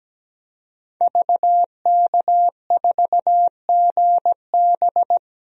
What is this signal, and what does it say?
Morse code, heard as a keyed tone, about 17 wpm, 700 Hz. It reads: VK4GB
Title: DXCC-100 Call Signs - 5 Characters - Repeated 2x - 17wpm